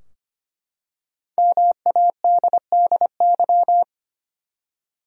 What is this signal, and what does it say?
Morse code, heard as a keyed tone, about 25 wpm, 700 Hz. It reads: MADDY
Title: USA Common Names - Max 5 Letters - 25wpm